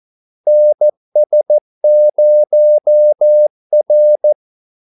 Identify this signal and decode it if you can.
Morse code, heard as a keyed tone, about 14 wpm, 600 Hz. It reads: NS0R
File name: Callsigns-14wpm